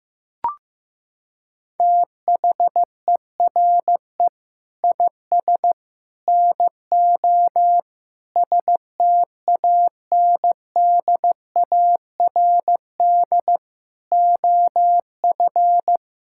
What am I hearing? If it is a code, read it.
Morse code, heard as a keyed tone, about 15 wpm, 700 Hz. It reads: THERE IS NO STANDARD OF
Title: Sentences from Top 1600 Words 15wpm